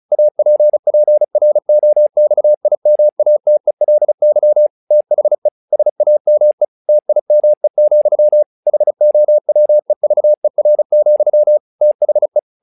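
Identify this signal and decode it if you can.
Morse code, heard as a keyed tone, about 34 wpm, 600 Hz. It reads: APPROXIMATELY THE SAME TIME, HOWEVER, THE
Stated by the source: Morse practice text "Tower Trouble" sent at 35 WPM, from the SKCC - Straight Key Century Club learning center audio files.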